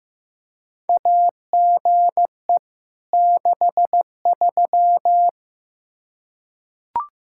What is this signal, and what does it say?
Morse code, heard as a keyed tone, about 15 wpm, 700 Hz. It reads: AGE 63
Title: Practice QSO - Exchange 3 - More Info - 15wpm